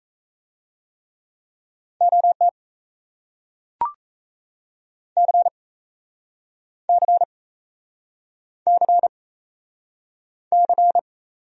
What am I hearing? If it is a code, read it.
Morse code, heard as a keyed tone, about 37 wpm, 700 Hz. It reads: OT C C C C